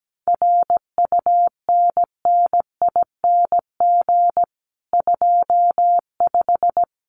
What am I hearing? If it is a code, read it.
Morse code, heard as a keyed tone, about 17 wpm, 700 Hz. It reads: RUNNING 25
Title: Common QSO Phrases 17wpm